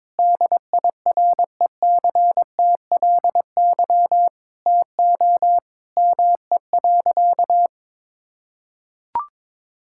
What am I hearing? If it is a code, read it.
Morse code, heard as a keyed tone, about 21 wpm, 700 Hz. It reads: DIRECTLY TO ME.
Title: Sentences from Top 1000 Words - Review Part 5 - 22wpm